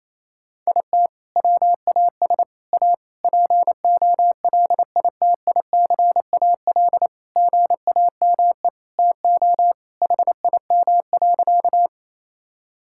Morse code, heard as a keyed tone, about 27 wpm, 700 Hz. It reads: IT WAS A POLITICAL GAME TO HIM.
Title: Sentences from Top 1000 Words - Review Part 5 - 28wpm